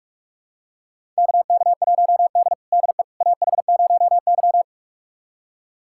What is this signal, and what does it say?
Morse code, heard as a keyed tone, about 41 wpm, 700 Hz. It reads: KK1D DE AH0Y
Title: USA Callsign DE Callsign - 4 Characters - 45wpm